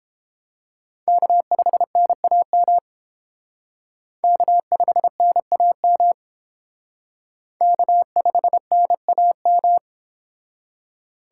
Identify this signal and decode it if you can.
Morse code, heard as a keyed tone, about 28 wpm, 700 Hz. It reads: K5NAM K5NAM K5NAM